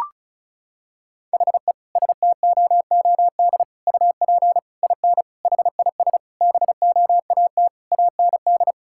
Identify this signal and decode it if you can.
Morse code, heard as a keyed tone, about 34 wpm, 700 Hz. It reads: HE STOOD UP IN HIS BOAT AND
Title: Sentences from Top 1600 Words 35wpm